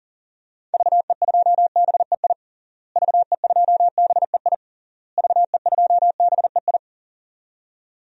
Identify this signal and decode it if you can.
Morse code, heard as a keyed tone, about 37 wpm, 700 Hz. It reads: VE2BEI VE2BEI VE2BEI
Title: DXCC-100 Call Signs - 6 Characters - Repeated 3x - 40wpm